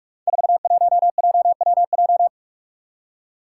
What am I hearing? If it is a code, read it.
Morse code, heard as a keyed tone, about 44 wpm, 700 Hz. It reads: 41JWJ
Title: Five Challenging Letter-Number Combinations 45wpm